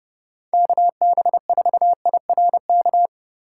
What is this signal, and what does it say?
Morse code, heard as a keyed tone, about 30 wpm, 700 Hz. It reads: KB4IRK
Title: USA Call Signs - 6 Characters - 4x Character Spacing - 30wpm